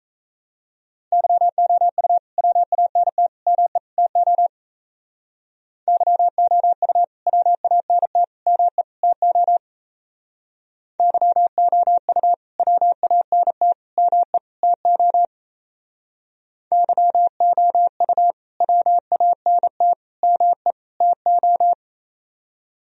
Morse code, heard as a keyed tone, about 34 wpm, 700 Hz. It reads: YOU WANT ME TO YOU WANT ME TO YOU WANT ME TO YOU WANT ME TO
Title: Speed-Racing - Sets of 4 Words 28wpm